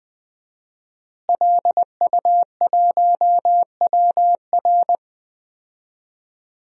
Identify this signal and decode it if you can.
Morse code, heard as a keyed tone, about 20 wpm, 700 Hz. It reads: LU1WR
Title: DXCC-100 Call Signs - 5 Characters - Repeated 3x - 20wpm